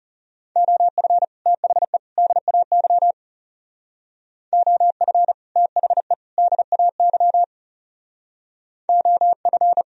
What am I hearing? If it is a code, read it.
Morse code, heard as a keyed tone, about 36 wpm, 700 Hz. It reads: OF THE DAY OF THE DAY OF
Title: Speed-Racing - Sets of 3 Words 30wpm